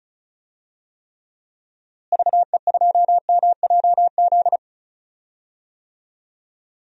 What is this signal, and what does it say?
Morse code, heard as a keyed tone, about 36 wpm, 700 Hz. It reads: VE2MJZ
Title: DXCC-100 Call Signs - 6 Characters - Repeated 2x - 35wpm